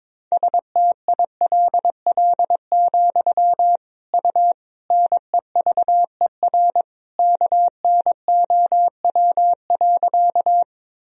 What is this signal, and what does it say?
Morse code, heard as a keyed tone, about 22 wpm, 700 Hz. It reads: STILL, U NEVER KNOW.
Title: Sentences from Top 300 Words and Abbreviations 22wpm